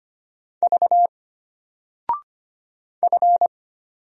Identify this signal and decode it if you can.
Morse code, heard as a keyed tone, about 25 wpm, 700 Hz. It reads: V F
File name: Single-Letter-Number-25wpm